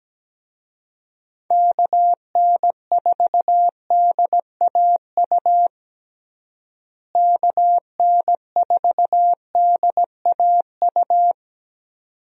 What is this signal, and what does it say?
Morse code, heard as a keyed tone, about 17 wpm, 700 Hz. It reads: KN4DAU KN4DAU
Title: USA Call Signs - 6 Characters - Repeated 2x - 17wpm